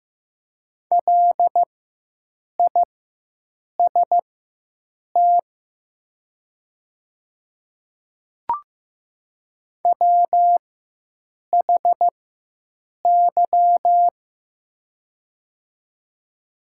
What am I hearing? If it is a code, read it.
Morse code, heard as a keyed tone, about 15 wpm, spaced as for 7 wpm, 700 Hz. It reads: LIST WHY
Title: 4x Character Spacing - Top 500 Words - 15wpm